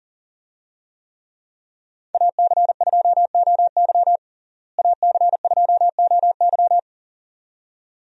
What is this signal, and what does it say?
Morse code, heard as a keyed tone, about 40 wpm, 700 Hz. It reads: AC2OY AC2OY
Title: USA Call Signs - 5 Characters - Repeated 2x - 40wpm